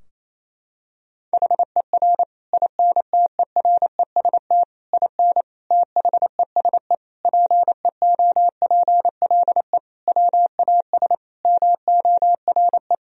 Morse code, heard as a keyed tone, about 28 wpm, 700 Hz. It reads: HER INTEREST IN THESE PEOPLE WAS MORE
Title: Sentences from Top 1000 Words - Review Part 2 - 28wpm